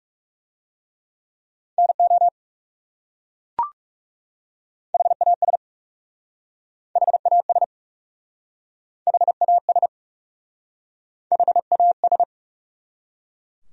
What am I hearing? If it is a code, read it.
Morse code, heard as a keyed tone, about 40 wpm, 700 Hz. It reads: NO HAS HAS HAS HAS